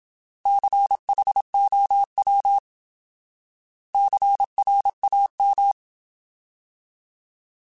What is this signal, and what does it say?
Morse code, heard as a keyed tone, about 27 wpm, 800 Hz. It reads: CHOW CRAM
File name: Alphabetical / Alphabetical mp3s / Cs @25